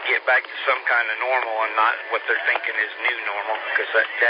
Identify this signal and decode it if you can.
Emergency Alert System (SAME) tones, heard as harsh buzzing bursts, with speech. It is heard from 0.8 to 4.1 s.